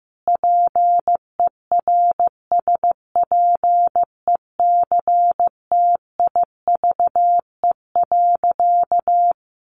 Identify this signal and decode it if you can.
Morse code, heard as a keyed tone, about 15 wpm, 700 Hz. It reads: PERSPECTIVE.
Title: Sentences from Top 1200 Words 15wpm